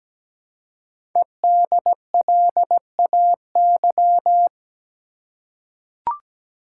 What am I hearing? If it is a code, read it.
Morse code, heard as a keyed tone, about 17 wpm, 700 Hz. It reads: EDLAY
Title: Code Groups - 5 Letters - 17wpm